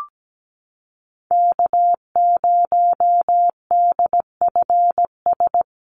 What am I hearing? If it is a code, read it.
Morse code, heard as a keyed tone, about 17 wpm, 700 Hz. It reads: K0DFS
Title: USA Call Signs Repeated 3x 17wpm